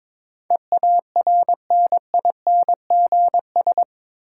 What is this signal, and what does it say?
Morse code, heard as a keyed tone, about 22 wpm, 700 Hz. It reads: EARNINGS
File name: practice / 22wpm / Eight Letter Words - 22wpm